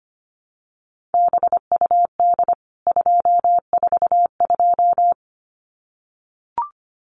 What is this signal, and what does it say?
Morse code, heard as a keyed tone, about 25 wpm, 700 Hz. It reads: BUD 242
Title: CWT Exchange 25wpm